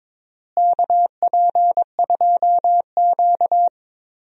Morse code, heard as a keyed tone, about 22 wpm, 700 Hz. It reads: KP2Q
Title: DXCC-100 Call Signs - 4 Characters - Repeated 2x - 22wpm